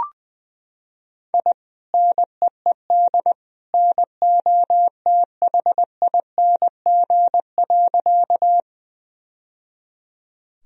Morse code, heard as a keyed tone, about 20 wpm, 700 Hz. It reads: I NEED NOTHING.
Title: Sentences from Top 1000 Words - Review Part 1 - 20wpm